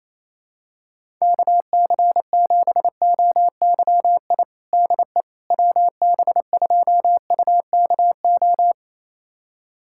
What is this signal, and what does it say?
Morse code, heard as a keyed tone, about 28 wpm, 700 Hz. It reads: KC7OYI DE WB2UKO
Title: USA Callsign DE Callsign - 6 Characters - 28wpm